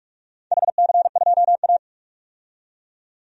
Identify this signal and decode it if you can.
Morse code, heard as a keyed tone, about 47 wpm, 700 Hz. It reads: HC2A
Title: DXCC-100 Call Signs - 4 Characters - 45wpm